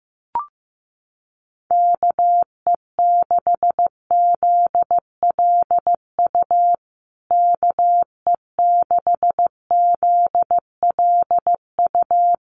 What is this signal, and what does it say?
Morse code, heard as a keyed tone, about 15 wpm, 700 Hz. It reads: KE6ZLU KE6ZLU
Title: USA Call Signs Repeated 2x 15wpm